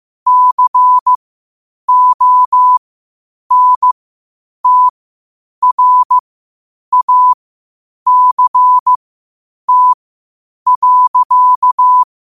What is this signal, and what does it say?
Morse code, heard as a keyed tone, about 15 wpm, spaced as for 8 wpm, 1000 Hz. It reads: CONTRACT.
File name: qotdfiles / 8wpm